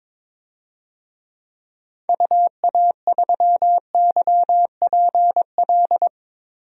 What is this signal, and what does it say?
Morse code, heard as a keyed tone, about 21 wpm, 700 Hz. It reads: UA3YPL
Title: DXCC-100 Call Signs - 6 Characters - 22wpm